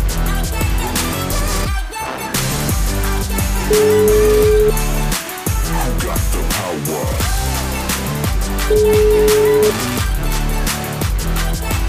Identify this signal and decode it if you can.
Ringback tone (Europe, 425 Hz), with noise.